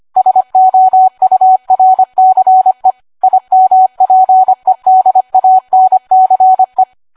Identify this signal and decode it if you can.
Morse code, heard as a keyed tone, about 25 wpm, 750 Hz. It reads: SOURCE IMPEDANCE